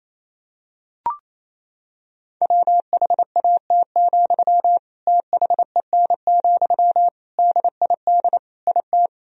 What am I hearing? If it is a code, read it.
Morse code, heard as a keyed tone, about 28 wpm, 700 Hz. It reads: WHAT, THEN, DID IT
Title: Sentences from Top 1000 Words - Review Part 4 - 28wpm